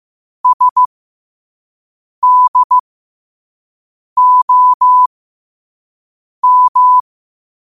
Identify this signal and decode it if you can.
Morse code, heard as a keyed tone, about 15 wpm, spaced as for 5 wpm, 1000 Hz. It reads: SDOM
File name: qotdfiles / 5wpm